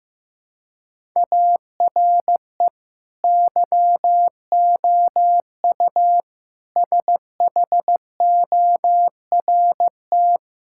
Morse code, heard as a keyed tone, about 15 wpm, 700 Hz. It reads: ARE YOU SHORT
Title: Sentences from Top 1000 Words - Review Part 5 - 15wpm